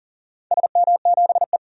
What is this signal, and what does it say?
Morse code, heard as a keyed tone, about 40 wpm, 700 Hz. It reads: SM7E
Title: DXCC-100 Call Signs - 4 Characters - 40wpm